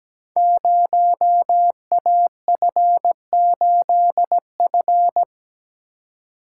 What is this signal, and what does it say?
Morse code, heard as a keyed tone, about 17 wpm, 700 Hz. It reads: 0AF8F
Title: Code Groups - 5 Letter:Numbers - 17wpm